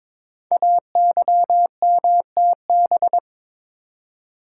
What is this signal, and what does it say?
Morse code, heard as a keyed tone, about 23 wpm, 700 Hz. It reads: AYMTB
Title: Code Groups - 5 Letters - 22wpm